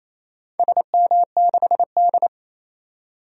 Morse code, heard as a keyed tone, about 28 wpm, 700 Hz. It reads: SM6D
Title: DXCC-100 Call Signs - 4 Characters - 28wpm